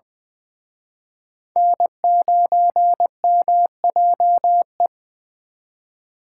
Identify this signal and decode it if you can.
Morse code, heard as a keyed tone, about 23 wpm, 700 Hz. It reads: N9MJE
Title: Speed Racing - USA 1x3 Call Signs 20wpm